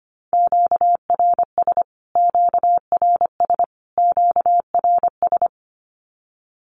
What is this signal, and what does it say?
Morse code, heard as a keyed tone, about 25 wpm, 700 Hz. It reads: QRS QRS QRS
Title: Q-Codes Repeated 3x 25wpm